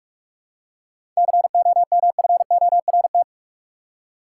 Morse code, heard as a keyed tone, about 41 wpm, 700 Hz. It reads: COMFORT
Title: Seven Letter Words - 45wpm